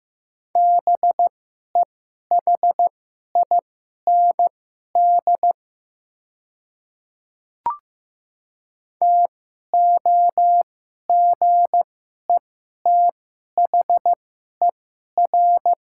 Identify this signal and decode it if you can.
Morse code, heard as a keyed tone, about 15 wpm, spaced as for 11 wpm, 700 Hz. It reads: BEHIND TOGETHER
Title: 2x Character Spacing - Top 500 Words - 15wpm